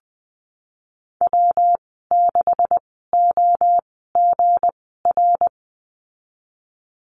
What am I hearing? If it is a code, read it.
Morse code, heard as a keyed tone, about 20 wpm, spaced as for 14 wpm, 700 Hz. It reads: W6OGR
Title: 238 Lesson 38g - Letter q - Call Signs 20wpm